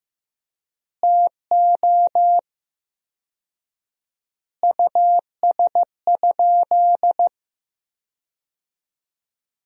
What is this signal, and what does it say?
Morse code, heard as a keyed tone, about 15 wpm, 700 Hz. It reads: TO US?
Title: 4x Word-Spacing - Sentences from Top 400 Words - 15wpm